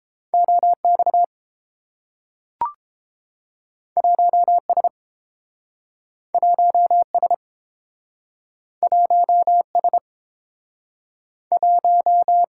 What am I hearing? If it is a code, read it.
Morse code, heard as a keyed tone, about 30 wpm, 700 Hz. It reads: O X 1 S 1 S 1 S 1